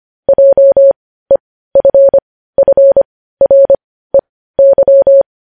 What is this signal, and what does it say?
Morse code, heard as a keyed tone, about 25 wpm, spaced as for 15 wpm, 550 Hz. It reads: JEFFREY
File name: podcast / News headlines-1x-25x15